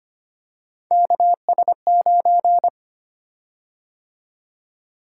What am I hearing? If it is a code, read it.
Morse code, heard as a keyed tone, about 25 wpm, 700 Hz. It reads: KS9
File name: Three Letter-Number Combinations - 25wpm